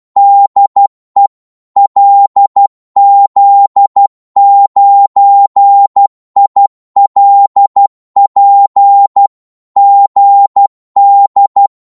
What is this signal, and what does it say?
Morse code, heard as a keyed tone, about 12 wpm, 800 Hz. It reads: DE LZ9ILP GD
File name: Morse / qso-12wpm